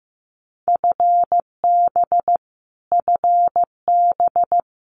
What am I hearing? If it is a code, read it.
Morse code, heard as a keyed tone, about 15 wpm, 700 Hz. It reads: FB FB
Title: Abbreviations Repeated 3x - 15wpm